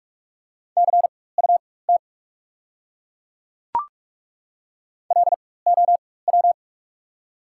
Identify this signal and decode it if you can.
Morse code, heard as a keyed tone, about 48 wpm, spaced as for 21 wpm, 690 Hz. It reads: CUT LOW
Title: 4x Character Spacing - Top 300 Words - 45wpm